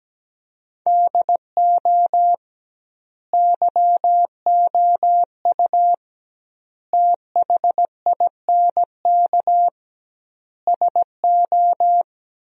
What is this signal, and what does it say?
Morse code, heard as a keyed tone, about 17 wpm, 700 Hz. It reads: DO YOU THINK SO
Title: 2x Word-Spacing - Sentences from Top 200 Words - 17wpm